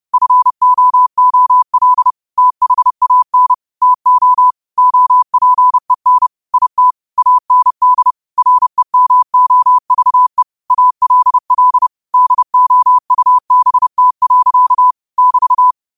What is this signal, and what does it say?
Morse code, heard as a keyed tone, about 30 wpm, 1000 Hz. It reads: FOOL THAN TO OPEN IT AND REMOVE ALL DOUBT. <BT>